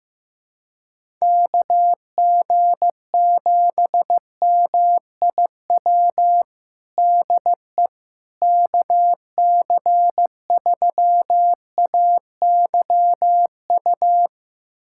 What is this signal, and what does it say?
Morse code, heard as a keyed tone, about 15 wpm, 700 Hz. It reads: KG7MIW DE KC3AYU
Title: USA Callsign DE Callsign 15wpm